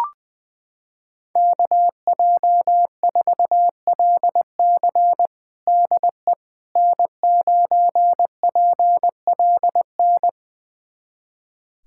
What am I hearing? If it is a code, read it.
Morse code, heard as a keyed tone, about 20 wpm, 700 Hz. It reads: KJ4LC DE N9PLN
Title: USA Callsign DE Callsign - 5 Characters - 20wpm